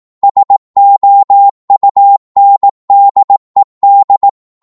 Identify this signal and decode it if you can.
Morse code, heard as a keyed tone, about 18 wpm, 800 Hz. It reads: SOUNDED